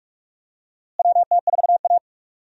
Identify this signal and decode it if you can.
Morse code, heard as a keyed tone, about 46 wpm, 700 Hz. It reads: WT4A
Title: USA Call Signs - 4 Characters - Repeated 3x - 45wpm